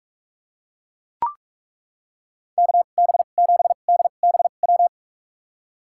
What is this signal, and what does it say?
Morse code, heard as a keyed tone, about 41 wpm, spaced as for 31 wpm, 700 Hz. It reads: KB7DBW
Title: USA Call Signs - 6 Characters - 2x Character Spacing - 45wpm